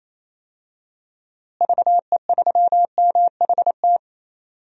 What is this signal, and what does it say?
Morse code, heard as a keyed tone, about 28 wpm, 700 Hz. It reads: VE3MHT
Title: DXCC-100 Call Signs - 6 Characters - Repeated 3x - 28wpm